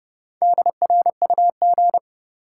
Morse code, heard as a keyed tone, about 30 wpm, 700 Hz. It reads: DRUG
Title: Sets of 5 Words - New Word Drill - 30wpm